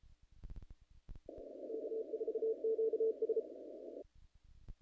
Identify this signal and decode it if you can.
Morse code, heard as a keyed tone, about 34 wpm, 450 Hz. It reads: K4QS
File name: contests / naqp-cw-k5tr-op-k5pi- -1900